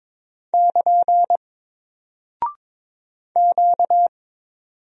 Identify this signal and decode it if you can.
Morse code, heard as a keyed tone, about 22 wpm, 700 Hz. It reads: <KN> Q
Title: Single Character (Letter:Number:Punctuation:Prosign) 22wpm